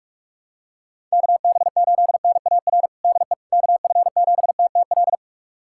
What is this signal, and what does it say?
Morse code, heard as a keyed tone, about 41 wpm, 700 Hz. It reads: KB8NAR DE KF7TTL
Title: USA Callsign DE Callsign - 6 Characters - 45wpm